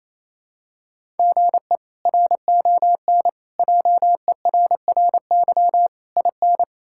Morse code, heard as a keyed tone, about 28 wpm, 700 Hz. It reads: GE RON JERRY IN
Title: K1USN SST QSO Practice - 28wpm